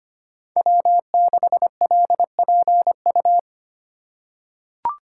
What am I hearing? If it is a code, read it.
Morse code, heard as a keyed tone, about 25 wpm, 700 Hz. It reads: W6LPU
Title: USA Call Signs - 2x Character Spacing - 25wpm